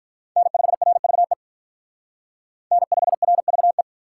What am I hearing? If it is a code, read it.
Morse code, heard as a keyed tone, about 44 wpm, 700 Hz. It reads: N5RVE N5RVE